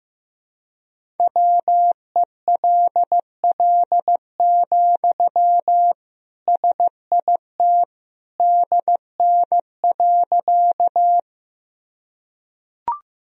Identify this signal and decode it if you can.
Morse code, heard as a keyed tone, about 15 wpm, 700 Hz. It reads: WELL, SIT DN.
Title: Sentences from Top 500 Words and Abbreviations 15wpm